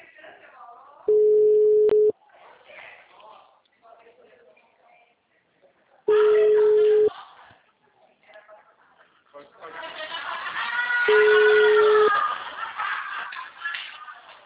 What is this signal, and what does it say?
Ringback tone (Europe, 425 Hz), with speech.